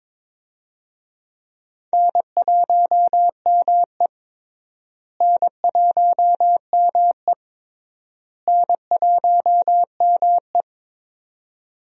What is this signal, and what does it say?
Morse code, heard as a keyed tone, about 22 wpm, 700 Hz. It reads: N1ME N1ME N1ME
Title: USA Call Signs - 4 Characters - Repeated 3x - 22wpm